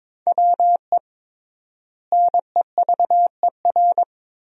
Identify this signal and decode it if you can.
Morse code, heard as a keyed tone, about 22 wpm, 700 Hz. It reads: WE NEVER
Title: 3x Word-Spacing - Sentences from Top 300 Words - 22wpm